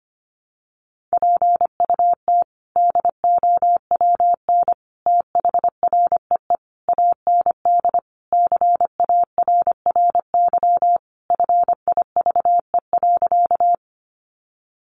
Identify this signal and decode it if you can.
Morse code, heard as a keyed tone, about 25 wpm, 700 Hz. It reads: PUT DOWN THREE AND CARRY FIVE.
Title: Sentences from Top 400 Words 25wpm